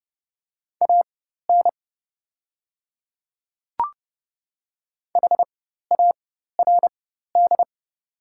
Morse code, heard as a keyed tone, about 30 wpm, spaced as for 14 wpm, 700 Hz. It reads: AN HARD
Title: 4x Character Spacing - Top 300 Words - 30wpm